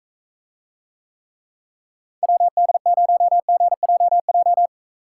Morse code, heard as a keyed tone, about 45 wpm, 700 Hz. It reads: WD0GJJ